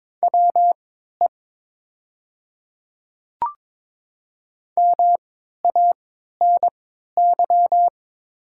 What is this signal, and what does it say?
Morse code, heard as a keyed tone, about 22 wpm, spaced as for 12 wpm, 700 Hz. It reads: WE MANY
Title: 3x Character Spacing - Top 100 Words - 22wpm